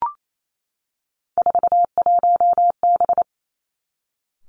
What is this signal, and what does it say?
Morse code, heard as a keyed tone, about 28 wpm, 690 Hz. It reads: 41B
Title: Three Challenging Letter-Number Combinations 28wpm